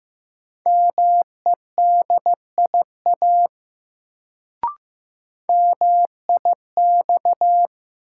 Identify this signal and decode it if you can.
Morse code, heard as a keyed tone, about 15 wpm, 700 Hz. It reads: MEDIA MIX
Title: Sentences from Top 1000 Words - Review Part 4 - New Word Drill - 15wpm